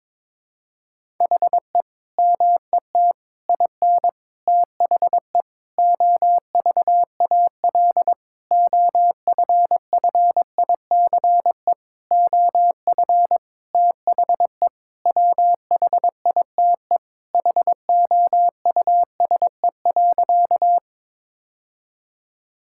Morse code, heard as a keyed tone, about 22 wpm, 700 Hz. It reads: HE MET IN THE OVAL OFFICE OF THE WHITE HOUSE.
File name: practice / 22wpm / Sentences with 8-Word Phrases - 22wpm